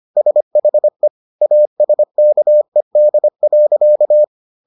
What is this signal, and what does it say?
Morse code, heard as a keyed tone, about 25 wpm, 600 Hz. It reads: SHE ASKED.